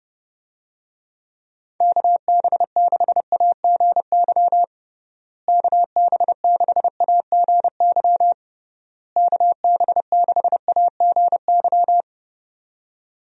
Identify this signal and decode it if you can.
Morse code, heard as a keyed tone, about 30 wpm, 700 Hz. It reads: KB6AGY KB6AGY KB6AGY
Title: USA Call Signs - 6 Characters - Repeated 3x - 30wpm